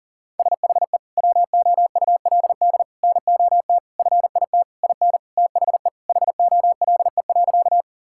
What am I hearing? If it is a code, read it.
Morse code, heard as a keyed tone, about 40 wpm, 700 Hz. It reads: SHE WOULD NOT FIT IN THE HOLE.
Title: Sentences from Top 1600 Words 40wpm